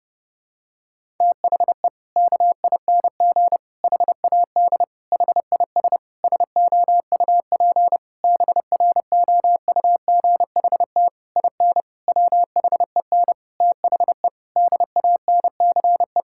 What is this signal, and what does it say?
Morse code, heard as a keyed tone, about 30 wpm, 700 Hz. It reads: THE KING HAD HIS SOUP BROUGHT IN WHEN THE DANCE
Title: Sentences from Top 1000 Words - Review Part 6 - 30wpm